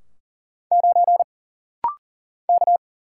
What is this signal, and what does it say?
Morse code, heard as a keyed tone, about 40 wpm, 700 Hz. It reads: 9 K
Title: Single Letter-Number - Rapid-Fire 40wpm